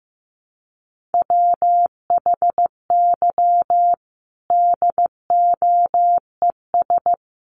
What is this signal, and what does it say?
Morse code, heard as a keyed tone, about 15 wpm, 700 Hz. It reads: WHY DOES
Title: Sentences from Top 700 Words 15wpm